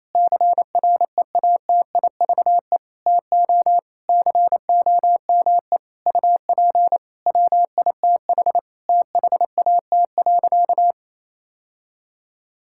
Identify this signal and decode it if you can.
Morse code, heard as a keyed tone, about 28 wpm, 700 Hz. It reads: CREATIVE TO COME UP WITH THAT.
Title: Sentences from Top 1000 Words - Review Part 5 - 28wpm